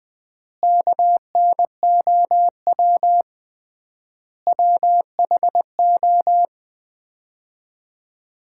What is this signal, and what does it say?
Morse code, heard as a keyed tone, about 20 wpm, 700 Hz. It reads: KNOW WHO
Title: 3x Word-Spacing - Sets of 5 Words - 20wpm